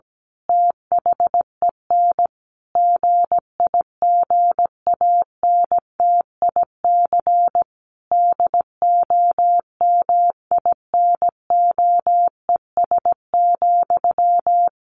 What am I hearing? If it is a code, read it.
Morse code, heard as a keyed tone, about 17 wpm, 700 Hz. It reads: THEN GIGANTIC DOMINOES,
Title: A Sound Of Thunder 17wpm